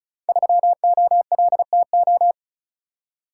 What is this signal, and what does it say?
Morse code, heard as a keyed tone, about 36 wpm, 700 Hz. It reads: 3OLTO
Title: Code Groups - 5 Letter:Numbers - 35wpm